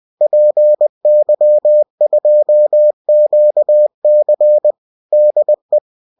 Morse code, heard as a keyed tone, about 20 wpm, 600 Hz. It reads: PY2QC DE